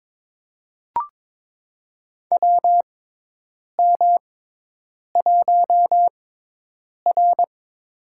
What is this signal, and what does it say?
Morse code, heard as a keyed tone, about 22 wpm, spaced as for 8 wpm, 700 Hz. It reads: WM1R
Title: USA Call Signs - 4 Characters - 6x Character Spacing - 22wpm